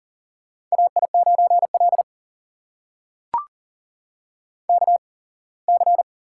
Morse code, heard as a keyed tone, about 40 wpm, spaced as for 24 wpm, 700 Hz. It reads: AI9L KC
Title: USA Call Signs - 8x Character Spacing - 40wpm